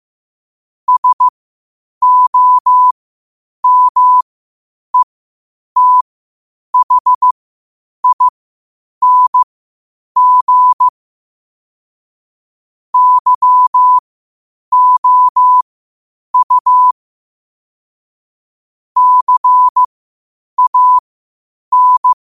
Morse code, heard as a keyed tone, about 15 wpm, spaced as for 8 wpm, 1000 Hz. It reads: SOMETHING YOU CAN
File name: qotdfiles / 8wpm